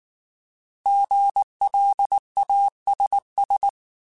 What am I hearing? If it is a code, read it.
Morse code, heard as a keyed tone, about 19 wpm, 800 Hz. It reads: GLASS